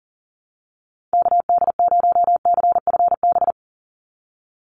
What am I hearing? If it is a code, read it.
Morse code, heard as a keyed tone, about 40 wpm, 700 Hz. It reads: KD0CFB
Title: USA Call Signs 40wpm